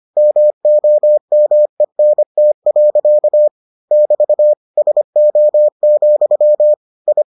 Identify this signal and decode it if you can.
Morse code, heard as a keyed tone, about 25 wpm, 600 Hz. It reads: MOMENT. <BT> SO, I